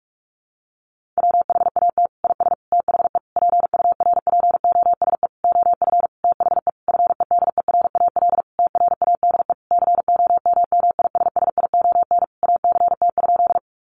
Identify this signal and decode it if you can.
Morse code, heard as a keyed tone, about 46 wpm, 700 Hz. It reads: WHAT IS THE PURPOSE OF THE FEDERAL TRADE COMMISSION ACT?
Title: Sentences from Top 1400 Words 45wpm